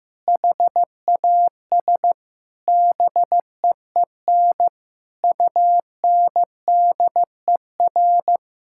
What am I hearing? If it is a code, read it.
Morse code, heard as a keyed tone, about 15 wpm, 700 Hz. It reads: HAS BEEN UNDER
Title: Sentences from Top 1000 Words - Review Part 6 - 15wpm